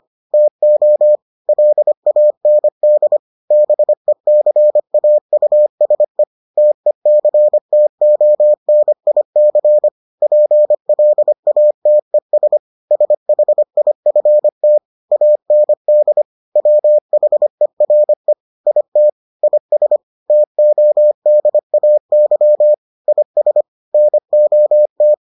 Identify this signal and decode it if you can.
Morse code, heard as a keyed tone, about 25 wpm, 600 Hz. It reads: TO LAND BECAUSE TECTONIC PLATES SHIFT AND WHERE IT IS TODAY IS NOT